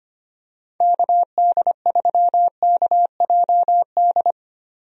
Morse code, heard as a keyed tone, about 25 wpm, 700 Hz. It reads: KD3KJD
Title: USA Call Signs - 6 Characters - Repeated 2x - 25wpm